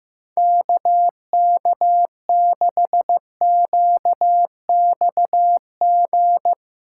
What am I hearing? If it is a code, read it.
Morse code, heard as a keyed tone, about 15 wpm, 700 Hz. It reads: KK6QXG
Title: USA Callsign DE Callsign - 6 Characters - 15wpm